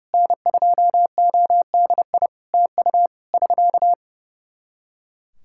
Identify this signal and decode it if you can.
Morse code, heard as a keyed tone, about 30 wpm, 700 Hz. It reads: N2ODI TU <SK>
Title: K1USN SST QSO Practice 30wpm